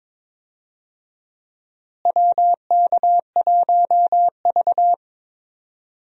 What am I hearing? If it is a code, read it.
Morse code, heard as a keyed tone, about 22 wpm, 700 Hz. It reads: WK1V
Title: USA Call Signs - 4 Characters - Repeated 2x - 22wpm